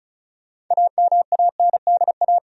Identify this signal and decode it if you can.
Morse code, heard as a keyed tone, about 34 wpm, 700 Hz. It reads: AMANDA
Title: USA Common Names - 35wpm